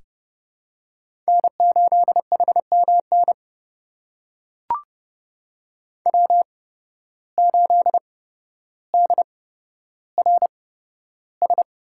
Morse code, heard as a keyed tone, about 30 wpm, spaced as for 13 wpm, 700 Hz. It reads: N8HMN W8DRS
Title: USA Call Signs - 5 Characters - 8x Character Spacing - 30wpm